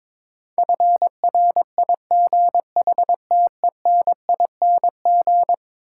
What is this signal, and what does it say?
Morse code, heard as a keyed tone, about 21 wpm, 700 Hz. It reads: FRIGHTENING
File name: practice / 22wpm / Eleven Letter Words - 22wpm